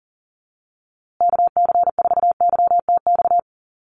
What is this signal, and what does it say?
Morse code, heard as a keyed tone, about 40 wpm, 700 Hz. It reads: KC4YTX
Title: USA Call Signs Repeated 3x 40wpm